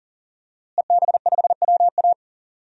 Morse code, heard as a keyed tone, about 39 wpm, 700 Hz. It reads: EB5WA
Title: DXCC-100 Call Signs - 5 Characters - Repeated 3x - 40wpm